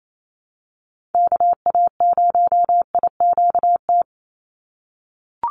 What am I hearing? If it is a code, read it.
Morse code, heard as a keyed tone, about 28 wpm, 700 Hz. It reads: KA0IQT
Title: Call Signs - 28wpm